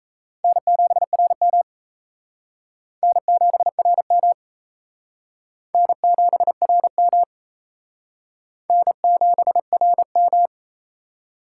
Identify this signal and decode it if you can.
Morse code, heard as a keyed tone, about 37 wpm, 700 Hz. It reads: N7RM N7RM N7RM N7RM